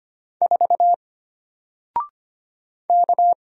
Morse code, heard as a keyed tone, about 25 wpm, 700 Hz. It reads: 4 K
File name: Single-Letter-Number-25wpm